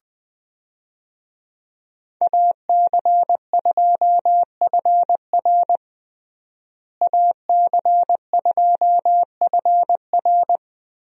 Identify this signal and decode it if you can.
Morse code, heard as a keyed tone, about 20 wpm, 700 Hz. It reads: AC2FR AC2FR
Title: USA Call Signs - 5 Characters - Repeated 3x - 20wpm